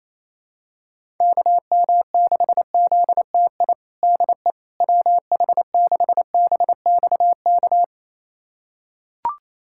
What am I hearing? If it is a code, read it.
Morse code, heard as a keyed tone, about 28 wpm, 700 Hz. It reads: KM6ZTI DE WH6BXK
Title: USA Callsign DE Callsign - 6 Characters - 28wpm